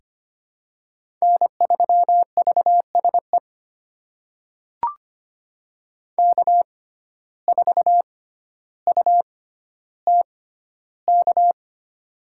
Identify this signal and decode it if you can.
Morse code, heard as a keyed tone, about 25 wpm, spaced as for 13 wpm, 700 Hz. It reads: N3VSE K4UTK
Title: USA Call Signs - 5 Characters - 6x Character Spacing - 25wpm